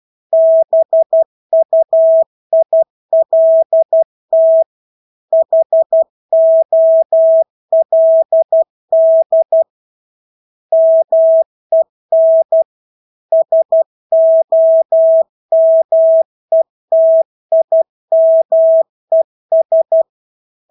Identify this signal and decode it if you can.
Morse code, heard as a keyed tone, about 12 wpm, 650 Hz. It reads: BUILT HOLD MEN SOMETIMES
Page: Engelske ord 12 wpm